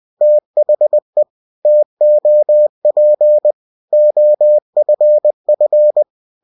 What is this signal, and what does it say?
Morse code, heard as a keyed tone, about 20 wpm, 600 Hz. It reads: THE TOP OFF